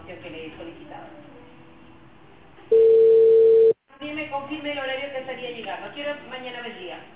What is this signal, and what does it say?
Ringback tone (China, 450 Hz), with speech.